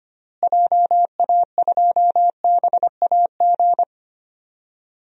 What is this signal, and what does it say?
Morse code, heard as a keyed tone, about 25 wpm, 700 Hz. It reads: JA2BAG
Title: DXCC-100 Call Signs - 6 Characters - Repeated 3x - 25wpm